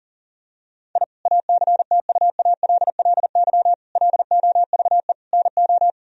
Morse code, heard as a keyed tone, about 40 wpm, 700 Hz. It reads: I ACTUALLY LOVE NO
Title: Sentences from Top 1000 Words - Review Part 4 - 40wpm